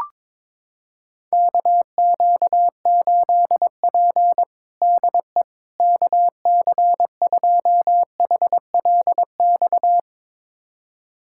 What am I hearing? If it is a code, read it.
Morse code, heard as a keyed tone, about 22 wpm, 700 Hz. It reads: KQ8P DE KC2HLX
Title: USA Callsign DE Callsign 22wpm